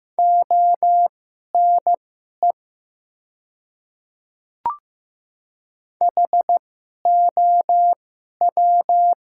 Morse code, heard as a keyed tone, about 15 wpm, spaced as for 11 wpm, 700 Hz. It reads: ONE HOW
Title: 2x Character Spacing - Top 200 Words - 15wpm